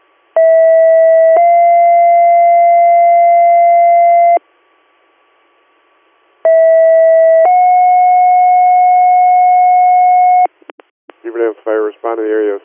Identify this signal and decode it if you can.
Two-tone page, heard as a one-second tone, then a three-second tone, about 650 then 690 Hz, about 650 then 730 Hz, with speech. It is heard from 0.4 to 10.5 s.